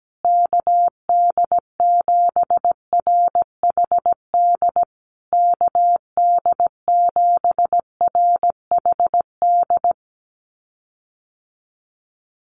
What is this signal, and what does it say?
Morse code, heard as a keyed tone, about 17 wpm, 700 Hz. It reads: KD7RHD KD7RHD
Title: USA Call Signs Repeated 3x 17wpm